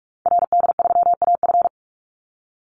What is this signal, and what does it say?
Morse code, heard as a keyed tone, about 44 wpm, 700 Hz. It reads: RD3AF
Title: Call Signs Repeated 2x - 45wpm